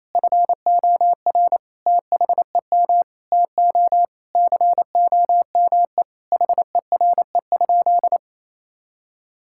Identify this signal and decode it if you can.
Morse code, heard as a keyed tone, about 28 wpm, 700 Hz. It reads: FOR THEM TO COME HERE?
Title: Sentences from Top 1000 Words - Review Part 5 - 28wpm